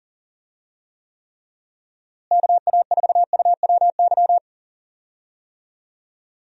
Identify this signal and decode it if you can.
Morse code, heard as a keyed tone, about 40 wpm, 700 Hz. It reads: KA4UWY
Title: USA Call Signs - 6 Characters - Repeated 3x - 40wpm